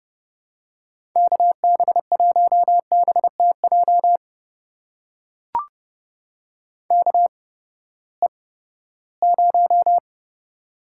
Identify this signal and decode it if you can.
Morse code, heard as a keyed tone, about 32 wpm, spaced as for 17 wpm, 700 Hz. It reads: KB1BTJ KE0
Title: USA Call Signs - 6 Characters - 8x Character Spacing - 30wpm